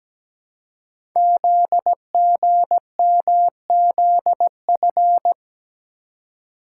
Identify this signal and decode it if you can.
Morse code, heard as a keyed tone, about 17 wpm, 700 Hz. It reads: ZGMZF
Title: Code Groups - 5 Letters - 17wpm